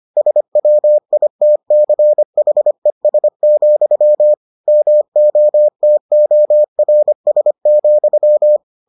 Morse code, heard as a keyed tone, about 25 wpm, 600 Hz. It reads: SWITCHES, MOTORS,